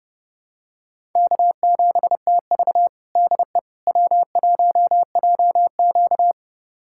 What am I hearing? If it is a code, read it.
Morse code, heard as a keyed tone, about 32 wpm, 700 Hz. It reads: K7TV DE W1JQ
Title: USA Callsign DE Callsign - 4 Characters - 30wpm